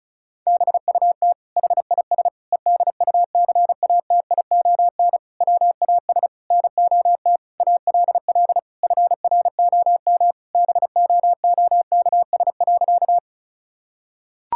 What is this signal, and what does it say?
Morse code, heard as a keyed tone, about 34 wpm, 700 Hz. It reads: BUT HIS EDUCATION WAS NOT ALL FROM BOOKS.
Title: Sentences from Top 1000 Words - Review Part 6 - 35wpm